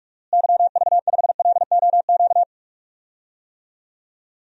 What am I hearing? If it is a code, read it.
Morse code, heard as a keyed tone, about 46 wpm, 700 Hz. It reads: YV5LOQ
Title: DXCC-100 Call Signs - 6 Characters - Repeated 2x - 45wpm